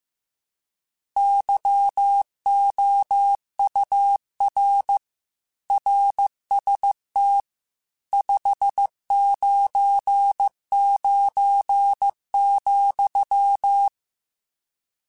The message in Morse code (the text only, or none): YOUR RST 599,